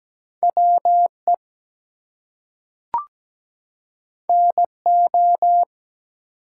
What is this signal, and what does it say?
Morse code, heard as a keyed tone, about 17 wpm, 700 Hz. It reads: WE NO
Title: Two Letter Words - 17wpm